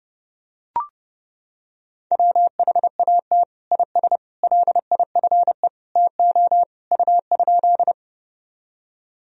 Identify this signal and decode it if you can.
Morse code, heard as a keyed tone, about 30 wpm, 700 Hz. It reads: WHAT IS LIFE TO U?
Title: Sentences from Top 300 Words and Abbreviations 30wpm